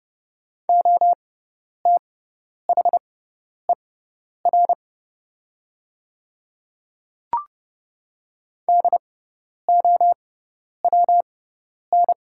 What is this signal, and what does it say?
Morse code, heard as a keyed tone, about 30 wpm, spaced as for 10 wpm, 700 Hz. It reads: OTHER DOWN
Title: 6x Character Spacing - Top 100 Words - 30wpm